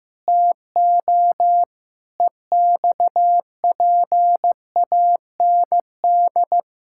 Morse code, heard as a keyed tone, about 15 wpm, 700 Hz. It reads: TO EXPAND
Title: Pangram Challenge v2 - 15wpm